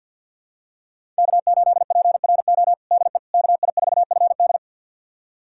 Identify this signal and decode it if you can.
Morse code, heard as a keyed tone, about 47 wpm, 700 Hz. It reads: K8PRO DE KI4FD
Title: USA Callsign DE Callsign - 5 Characters - 50wpm